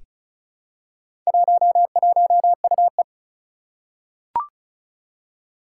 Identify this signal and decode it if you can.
Morse code, heard as a keyed tone, about 36 wpm, 700 Hz. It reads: 11UE
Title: Four Challenging Letter-Number Combinations 35wpm